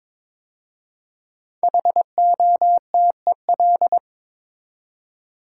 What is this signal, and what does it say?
Morse code, heard as a keyed tone, about 23 wpm, 700 Hz. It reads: HOTEL
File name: 5 Letter Words - First 3 Letters Given - 22wpm